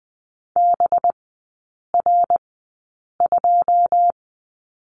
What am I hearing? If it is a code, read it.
Morse code, heard as a keyed tone, about 20 wpm, spaced as for 8 wpm, 700 Hz. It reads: BR2
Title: 205 Lesson 33e - Letter j - Three Characters 20wpm